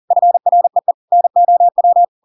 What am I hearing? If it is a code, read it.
Morse code, heard as a keyed tone, about 40 wpm, 700 Hz. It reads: FREE NOW
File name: Pinocchio / Pinocchio ch4 40 40 ik1hge